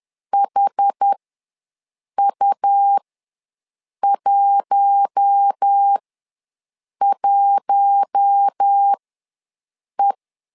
Morse code, heard as a keyed tone, about 10 wpm, spaced as for 6 wpm, 800 Hz. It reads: HU11E
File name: LEZIONI / 4 Q1 ws 10wpm 2538